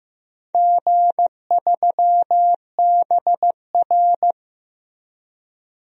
Morse code, heard as a keyed tone, about 15 wpm, 700 Hz. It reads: G3BR
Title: DXCC-100 Call Signs - 4 Characters - Repeated 2x - 15wpm